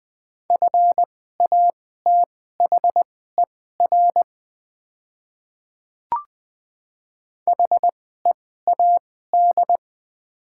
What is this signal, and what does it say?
Morse code, heard as a keyed tone, about 20 wpm, spaced as for 14 wpm, 700 Hz. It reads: FATHER HEAD
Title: 2x Character Spacing - Top 500 Words - 20wpm